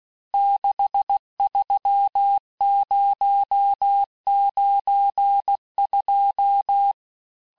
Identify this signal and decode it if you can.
Morse code, heard as a keyed tone, about 16 wpm, 800 Hz. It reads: 63092
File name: cours-du-lundi files / cours-1592--teste-chiffre-vitesse-15mpm